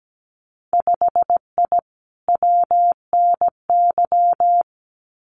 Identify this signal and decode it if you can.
Morse code, heard as a keyed tone, about 17 wpm, 700 Hz. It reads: 5I WNY
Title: Winter Field Day Exchanges Encore 17wpm